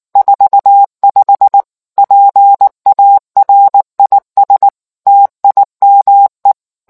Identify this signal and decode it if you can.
Morse code, heard as a keyed tone, about 19 wpm, 800 Hz. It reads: 45 PARIS TIME